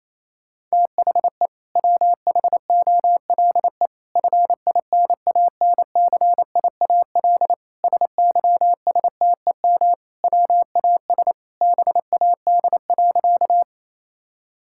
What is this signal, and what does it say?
Morse code, heard as a keyed tone, about 27 wpm, 700 Hz. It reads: THE WHOLE FINANCIAL SYSTEM WAS BAD.
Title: Sentences from Top 1000 Words - Review Part 5 - 28wpm